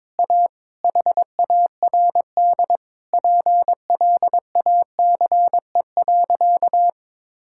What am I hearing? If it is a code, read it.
Morse code, heard as a keyed tone, about 22 wpm, 700 Hz. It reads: A HARD PLACE.
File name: Sentences with 7-Word Phrases - 22wpm